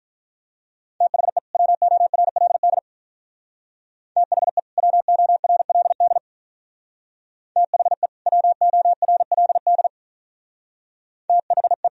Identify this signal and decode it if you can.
Morse code, heard as a keyed tone, about 43 wpm, 700 Hz. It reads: THE WORLD THE WORLD THE WORLD THE
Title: Speed-Racing - Sets of 2 Words 35wpm